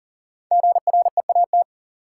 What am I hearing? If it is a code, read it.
Morse code, heard as a keyed tone, about 40 wpm, 700 Hz. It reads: GREAT
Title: Sentences from Top 200 Words - New Word Drill - 40wpm